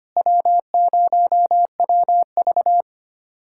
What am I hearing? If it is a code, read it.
Morse code, heard as a keyed tone, about 25 wpm, 700 Hz. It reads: W0WV
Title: USA Callsign DE Callsign - 4 Characters - 25wpm